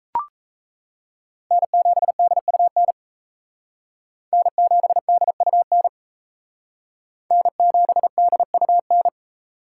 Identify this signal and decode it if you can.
Morse code, heard as a keyed tone, about 35 wpm, 700 Hz. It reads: N7DUN N7DUN N7DUN